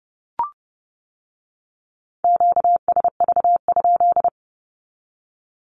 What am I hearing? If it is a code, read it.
Morse code, heard as a keyed tone, about 30 wpm, 700 Hz. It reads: QSV?
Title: Q-Codes with Spoken Meaning - 30wpm